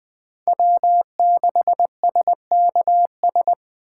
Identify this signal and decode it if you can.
Morse code, heard as a keyed tone, about 21 wpm, 700 Hz. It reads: W6SKS
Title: Speed Racing - USA 1x3 Call Signs 17wpm